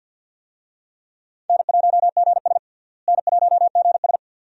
Morse code, heard as a keyed tone, about 47 wpm, 700 Hz. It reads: N1GS N1GS
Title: USA Call Signs - 4 Characters - Repeated 2x - 50wpm